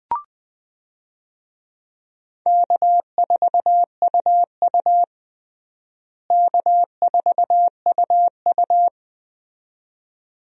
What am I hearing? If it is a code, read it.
Morse code, heard as a keyed tone, about 20 wpm, 700 Hz. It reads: K4UU K4UU
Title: USA Call Signs - 4 Characters - Repeated 2x - 20wpm